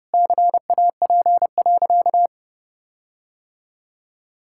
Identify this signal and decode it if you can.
Morse code, heard as a keyed tone, about 32 wpm, 700 Hz. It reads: CAP.
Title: Sentences from Top 1600 Words 30wpm